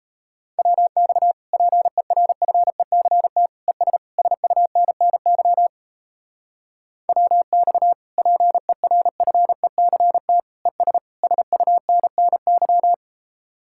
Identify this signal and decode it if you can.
Morse code, heard as a keyed tone, about 34 wpm, 700 Hz. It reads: WX PERFECT ES SUNNY WX PERFECT ES SUNNY
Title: Speed-Racing Common QSO Phrases Encore - 28wpm